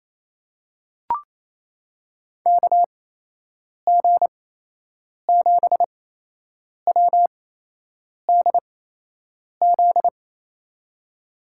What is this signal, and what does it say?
Morse code, heard as a keyed tone, about 28 wpm, spaced as for 8 wpm, 700 Hz. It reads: KG7WDZ
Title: USA Call Signs - 6 Characters - 8x Character Spacing - 28wpm